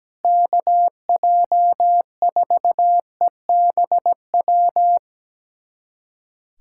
Morse code, heard as a keyed tone, about 17 wpm, 700 Hz. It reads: KJ4EBW
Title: USA Callsign DE Callsign 17wpm